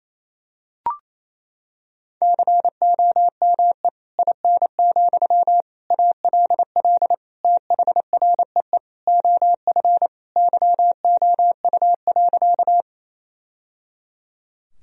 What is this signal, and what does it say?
Morse code, heard as a keyed tone, about 27 wpm, 700 Hz. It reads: COME IN, ALL THREE OF YOU.
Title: Sentences from Top 1000 Words - Review Part 1 - 28wpm